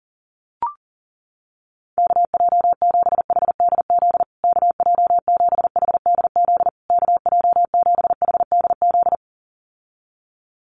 Morse code, heard as a keyed tone, about 40 wpm, 700 Hz. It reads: KJ7HDZ KJ7HDZ KJ7HDZ
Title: USA Call Signs Repeated 3x 40wpm